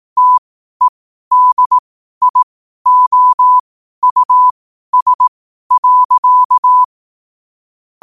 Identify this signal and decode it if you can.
Morse code, heard as a keyed tone, about 18 wpm, spaced as for 12 wpm, 1000 Hz. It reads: TEDIOUS.
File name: qotdfiles / 12wpm